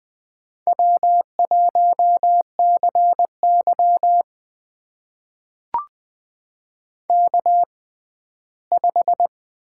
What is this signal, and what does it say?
Morse code, heard as a keyed tone, about 20 wpm, spaced as for 14 wpm, 700 Hz. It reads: W1CY K5
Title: USA Call Signs - 6x Character Spacing - 20wpm